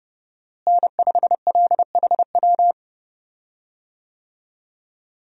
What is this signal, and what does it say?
Morse code, heard as a keyed tone, about 28 wpm, 700 Hz. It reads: N5LHW
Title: USA Call Signs - 5 Characters - Repeated 3x - 30wpm